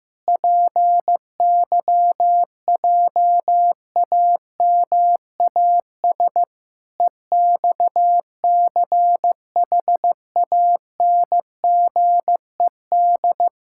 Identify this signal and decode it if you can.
Morse code, heard as a keyed tone, about 16 wpm, 700 Hz. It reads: PYJAMAS EXCHANGED
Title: Pangram Challenge v2 - 15wpm